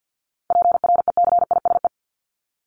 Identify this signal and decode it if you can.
Morse code, heard as a keyed tone, about 50 wpm, 700 Hz. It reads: PRECISE
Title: LY Suffix - Top 100 Words with Base Words 50wpm